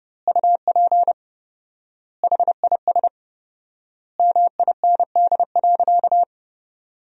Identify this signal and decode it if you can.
Morse code, heard as a keyed tone, about 30 wpm, 700 Hz. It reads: UP HIS MIND.
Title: 4x Word-Spacing - Sentences from Top 500 Words - 30wpm